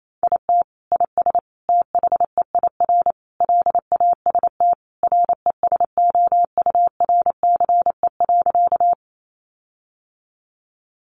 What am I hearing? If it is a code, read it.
Morse code, heard as a keyed tone, about 28 wpm, 700 Hz. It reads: IT IS THEIR LAST RESOURCE.
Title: Sentences from Top 1300 Words 28wpm